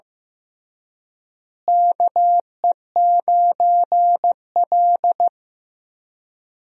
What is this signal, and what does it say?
Morse code, heard as a keyed tone, about 15 wpm, 700 Hz. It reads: KE9L
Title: USA Call Signs - 4 Characters - Repeated 3x - 15wpm